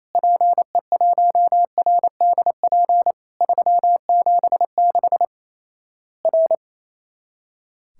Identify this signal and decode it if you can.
Morse code, heard as a keyed tone, about 28 wpm, 700 Hz. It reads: PE1RDP 376 R
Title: Worked All Europe DX Contest - QTC Exchange - 28wpm